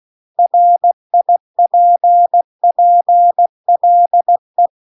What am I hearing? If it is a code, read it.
Morse code, heard as a keyed tone, about 16 wpm, 700 Hz. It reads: RIPPLE